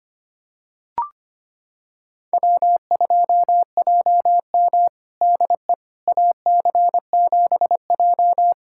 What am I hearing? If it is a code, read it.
Morse code, heard as a keyed tone, about 25 wpm, 700 Hz. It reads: W2JM DE AC7J
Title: USA Callsign DE Callsign 25wpm